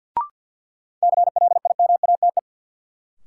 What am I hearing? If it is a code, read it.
Morse code, heard as a keyed tone, about 53 wpm, 700 Hz. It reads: CLIMATE
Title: Sentences from Top 1600 Words - New Word Drill Alt - 50wpm